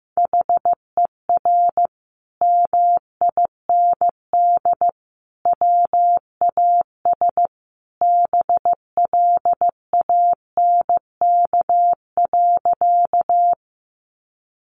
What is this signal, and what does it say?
Morse code, heard as a keyed tone, about 15 wpm, 700 Hz. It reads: HER MIND WAS BLANK.
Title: Sentences from Top 1500 Words 15wpm